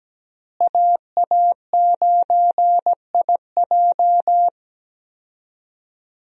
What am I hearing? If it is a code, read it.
Morse code, heard as a keyed tone, about 17 wpm, 700 Hz. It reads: AA9IJ
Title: USA Call Signs - 5 Characters - Repeated 2x - 17wpm